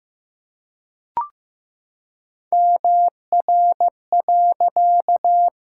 Morse code, heard as a keyed tone, about 15 wpm, 700 Hz. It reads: MR.
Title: Pangram Challenge v2 - 15wpm